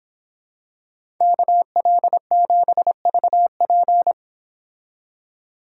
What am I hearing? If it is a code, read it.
Morse code, heard as a keyed tone, about 27 wpm, 700 Hz. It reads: KL7VP
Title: Speed Racing - USA 2x2 Call Signs 22wpm